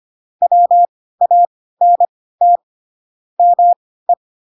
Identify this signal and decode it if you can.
Morse code, heard as a keyed tone, about 25 wpm, spaced as for 16 wpm, 700 Hz. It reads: WANT ME
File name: Pinocchio / Pinocchio ch7 10 25 ik1hge